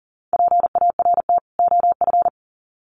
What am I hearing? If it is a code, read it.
Morse code, heard as a keyed tone, about 39 wpm, 700 Hz. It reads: PART OF
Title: Sets of 2 Words 40wpm